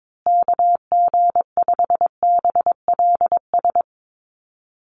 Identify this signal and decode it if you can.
Morse code, heard as a keyed tone, about 22 wpm, 700 Hz. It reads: KG5BLS
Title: USA Call Signs Repeated 2x 22wpm